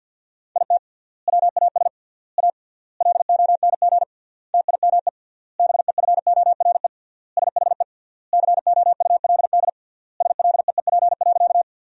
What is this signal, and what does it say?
Morse code, heard as a keyed tone, about 50 wpm, 700 Hz. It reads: IT WAS A LONG TIME BEFORE SHE COULD SLEEP.
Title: 3x Word-Spacing - Sentences from Top 500 Words - 50wpm